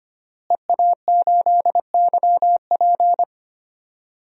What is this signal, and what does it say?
Morse code, heard as a keyed tone, about 25 wpm, 700 Hz. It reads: EA8YP
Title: DXCC-100 Call Signs - 5 Characters - Repeated 3x - 25wpm